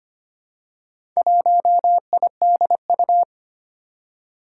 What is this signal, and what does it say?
Morse code, heard as a keyed tone, about 25 wpm, 700 Hz. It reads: 1IDU
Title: Four Challenging Letter-Number Combinations 25wpm